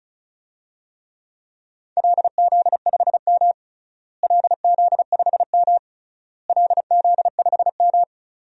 Morse code, heard as a keyed tone, about 34 wpm, 700 Hz. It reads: LZ5M LZ5M LZ5M
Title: DXCC-100 Call Signs - 4 Characters - Repeated 3x - 35wpm